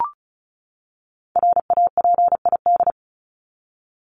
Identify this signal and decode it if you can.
Morse code, heard as a keyed tone, about 34 wpm, 700 Hz. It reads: RAPID
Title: LY Suffix - Top 100 Words with Base Words 35wpm